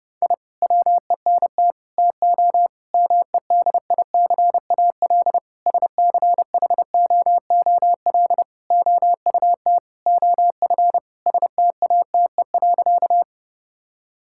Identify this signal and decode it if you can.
Morse code, heard as a keyed tone, about 32 wpm, 700 Hz. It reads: I WENT TO MEDICAL SCHOOL OUT OF STATE.
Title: Sentences from Top 1000 Words - Review Part 5 - 30wpm